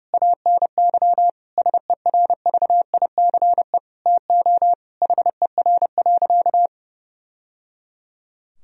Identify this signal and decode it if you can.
Morse code, heard as a keyed tone, about 30 wpm, 700 Hz. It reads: ANY SERVICE TO HER.
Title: Sentences from Top 1000 Words - Review Part 4 - 30wpm